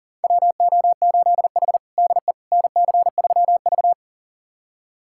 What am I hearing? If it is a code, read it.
Morse code, heard as a keyed tone, about 37 wpm, 700 Hz. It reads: WO8H DE NC3V
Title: USA Callsign DE Callsign - 4 Characters - 40wpm